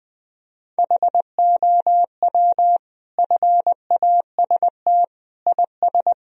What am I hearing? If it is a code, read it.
Morse code, heard as a keyed tone, about 21 wpm, 700 Hz. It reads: HOW FAST IS
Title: Sentences from Top 1000 Words - Review Part 4 - 20wpm